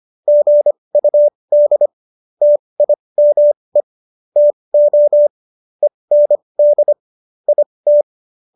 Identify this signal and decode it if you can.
Morse code, heard as a keyed tone, about 25 wpm, spaced as for 20 wpm, 590 Hz. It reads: GUD TIME TO END IT